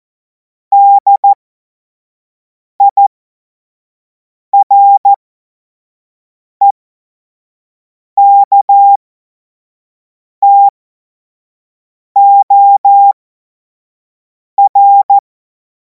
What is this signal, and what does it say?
Morse code, heard as a keyed tone, about 14 wpm, spaced as for 5 wpm, 800 Hz. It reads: DIREKTOR
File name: Morsekurs / Witze / Witz 31. März 2026 25 F70